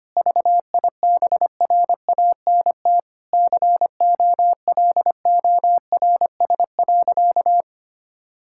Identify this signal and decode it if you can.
Morse code, heard as a keyed tone, about 25 wpm, 700 Hz. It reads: VIBRANT COLORS.
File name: practice / 25wpm / Difficult Sentences with Hints - 25wpm